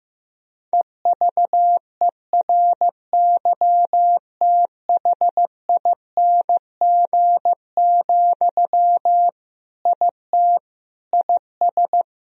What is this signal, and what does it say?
Morse code, heard as a keyed tone, about 15 wpm, 700 Hz. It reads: EVERYTHING, IT IS
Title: Sentences from Top 1000 Words - Review Part 1 - 15wpm